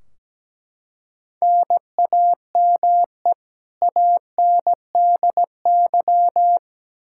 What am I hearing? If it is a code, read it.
Morse code, heard as a keyed tone, about 17 wpm, 700 Hz. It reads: NAME ANDY
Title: Practice QSO - Exchange 1 - Beginning - 17wpm